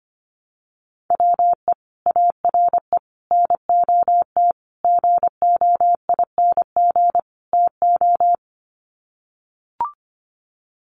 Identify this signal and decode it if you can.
Morse code, heard as a keyed tone, about 25 wpm, 700 Hz. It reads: WE ARE NOT GOING TO
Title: Sets of 5 Words 25wpm